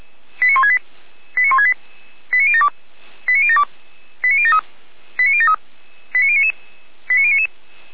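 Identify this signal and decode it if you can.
Five-tone selective call (ZVEI), heard as quick runs of five tones, tones of about 70 ms. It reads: 89168 89168 89072 89072 89073 89073 89099
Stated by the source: ZVEI five-tone selective call received by SDR.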